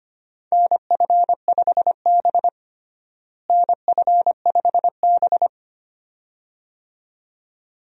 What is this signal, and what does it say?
Morse code, heard as a keyed tone, about 25 wpm, 700 Hz. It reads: NF5B NF5B
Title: USA Call Signs - 4 Characters - Repeated 2x - 25wpm